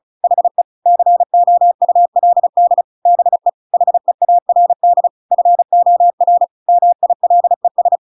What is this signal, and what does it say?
Morse code, heard as a keyed tone, about 34 wpm, 700 Hz. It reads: HE COULD BE HEARD FOR MILES
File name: Pinocchio / Pinocchio ch7 35 35 ik1hge